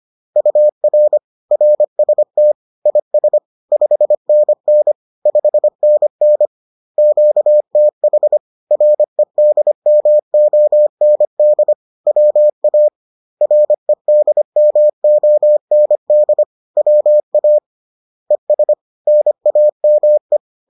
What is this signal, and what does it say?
Morse code, heard as a keyed tone, about 25 wpm, 600 Hz. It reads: UR RST IS 5NN 5NN QTH REDMOND WA REDMOND WA ES NAME